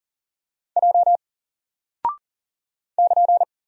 Morse code, heard as a keyed tone, about 40 wpm, 700 Hz. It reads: J <KN>
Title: Single Character (Letter:Number:Punctuation:Prosign) 40wpm